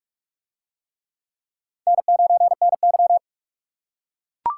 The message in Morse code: N9NY

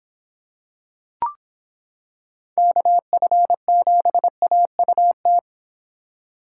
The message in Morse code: KF7AUT

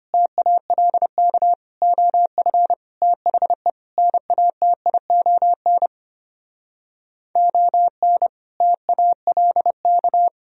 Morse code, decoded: TALK OF THE NATION ON TALK

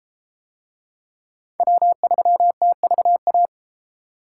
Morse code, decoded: W3TVA

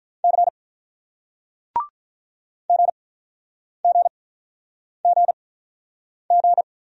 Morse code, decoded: C G G G G

